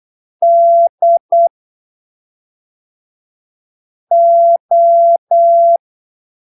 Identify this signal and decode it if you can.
Morse code, heard as a keyed tone, about 8 wpm, spaced as for 3 wpm, 680 Hz. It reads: DO
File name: arquivos / texto / texto 13 08ppm